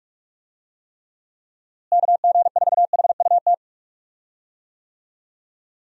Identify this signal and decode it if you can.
Morse code, heard as a keyed tone, about 43 wpm, 700 Hz. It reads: KG4HUT
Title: USA Call Signs - 6 Characters - Repeated 3x - 45wpm